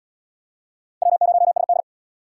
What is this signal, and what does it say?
Morse code, heard as a keyed tone, about 77 wpm, 700 Hz. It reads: W0SD